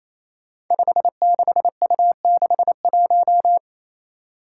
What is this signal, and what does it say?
Morse code, heard as a keyed tone, about 27 wpm, 700 Hz. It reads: 56U61
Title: Five Challenging Letter-Number Combinations 28wpm